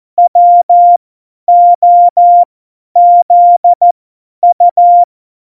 Morse code, heard as a keyed tone, about 14 wpm, spaced as for 10 wpm, 700 Hz. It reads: WOZU
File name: Morsekurs / Witze / Witz 15. Dezember 2020, 50 F70